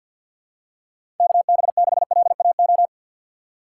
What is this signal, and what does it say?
Morse code, heard as a keyed tone, about 53 wpm, 700 Hz. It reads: K B 6 L A O
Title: USA Call Signs - 6 Characters - 8x Character Spacing - 50wpm